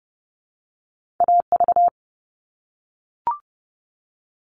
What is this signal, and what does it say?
Morse code, heard as a keyed tone, about 30 wpm, 700 Hz. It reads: AV
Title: All Cut Numbers - 2-Digit 30wpm